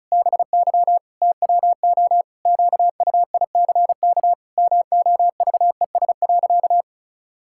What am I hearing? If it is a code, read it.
Morse code, heard as a keyed tone, about 34 wpm, 700 Hz. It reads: BY TWO QUICK MOVES.
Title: Pangram Challenge v2 - 35wpm